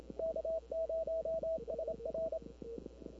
Morse code, heard as a keyed tone, about 27 wpm, 650 Hz, with noise. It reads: K0SR
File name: CW / 2005-k5tr arrl cw dx -20:00